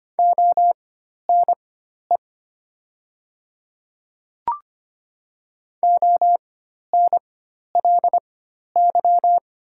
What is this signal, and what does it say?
Morse code, heard as a keyed tone, about 25 wpm, spaced as for 12 wpm, 700 Hz. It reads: ONE ONLY
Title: 4x Character Spacing - Top 200 Words - 25wpm